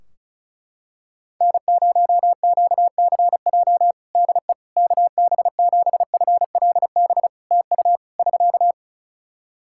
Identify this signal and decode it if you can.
Morse code, heard as a keyed tone, about 36 wpm, 700 Hz. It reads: N0QCJ DE KB7FLB TU <SK>
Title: Practice QSO - Exchange 4 - Ending - 35wpm